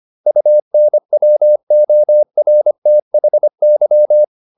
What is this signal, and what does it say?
Morse code, heard as a keyed tone, about 25 wpm, 600 Hz. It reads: UNWORTHY